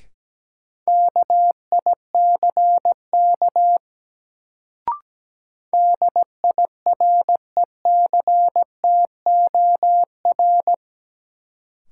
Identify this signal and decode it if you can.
Morse code, heard as a keyed tone, about 17 wpm, 700 Hz. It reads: KICK DIRECTOR
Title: Sentences from Top 1300 Words - New Word Drill - 17wpm